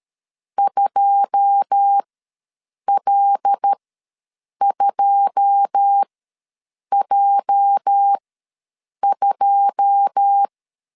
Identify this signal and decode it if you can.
Morse code, heard as a keyed tone, about 13 wpm, spaced as for 7 wpm, 800 Hz. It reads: 2L2J2
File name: LEZIONI / 9 2 ws 12wpm 3406